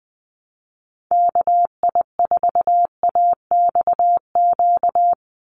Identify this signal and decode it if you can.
Morse code, heard as a keyed tone, about 20 wpm, 700 Hz. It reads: KI4AXQ
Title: USA Call Signs 20wpm